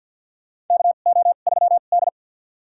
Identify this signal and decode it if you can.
Morse code, heard as a keyed tone, about 51 wpm, spaced as for 36 wpm, 700 Hz. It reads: KO3D
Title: USA Call Signs - 4 Characters - 2x Character Spacing - 50wpm